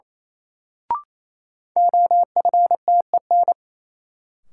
Morse code, heard as a keyed tone, about 28 wpm, 700 Hz. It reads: OFTEN
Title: Sentences from Top 300 Words - New Word Drill - 28wpm